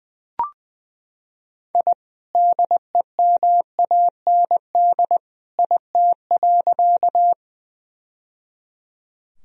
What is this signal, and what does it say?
Morse code, heard as a keyed tone, about 20 wpm, 700 Hz. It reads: I DEMAND IT.
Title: Sentences from Top 1000 Words - Review Part 4 - 20wpm